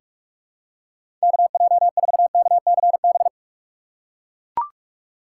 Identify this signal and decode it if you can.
Morse code, heard as a keyed tone, about 41 wpm, 700 Hz. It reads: KJ4KCB
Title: USA Call Signs - 6 Characters - 2x Character Spacing - 45wpm